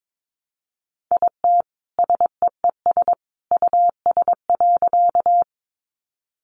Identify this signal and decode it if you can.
Morse code, heard as a keyed tone, about 22 wpm, 700 Hz. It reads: IT SEES US.
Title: A Sound Of Thunder 22wpm